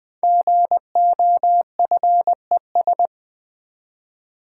Morse code, heard as a keyed tone, about 20 wpm, 700 Hz. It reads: GOFES
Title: Code Groups - 5 Letters - 20wpm